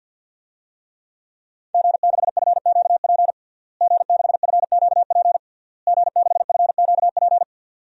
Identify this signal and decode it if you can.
Morse code, heard as a keyed tone, about 52 wpm, 700 Hz. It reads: G6FQP G6FQP G6FQP
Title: DXCC-100 Call Signs - 5 Characters - Repeated 3x - 50wpm